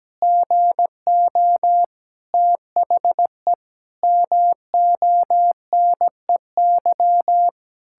Morse code, heard as a keyed tone, about 17 wpm, 700 Hz. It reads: GO THE MONEY
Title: Pangram Challenge v2 - 17wpm